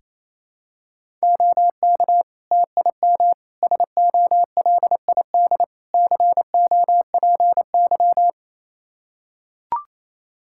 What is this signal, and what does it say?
Morse code, heard as a keyed tone, about 27 wpm, 700 Hz. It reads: OK TIM SOLID COPY
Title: Practice QSO - Exchange 3 - More Info - 28wpm